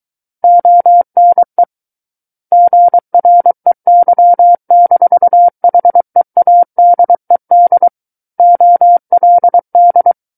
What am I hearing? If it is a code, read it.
Morse code, heard as a keyed tone, about 23 wpm, 700 Hz. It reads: ONE GREY-HEADED OLD
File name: Book One / 14-Chapter 14 - 23 WPM